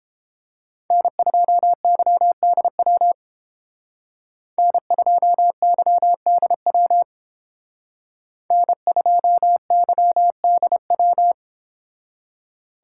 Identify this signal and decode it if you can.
Morse code, heard as a keyed tone, about 30 wpm, 700 Hz. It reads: N2YDW N2YDW N2YDW